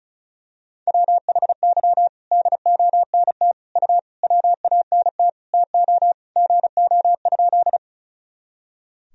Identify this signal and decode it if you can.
Morse code, heard as a keyed tone, about 36 wpm, 700 Hz. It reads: WHY DONT U WANT TO GO?
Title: Sentences from Top 300 Words and Abbreviations 35wpm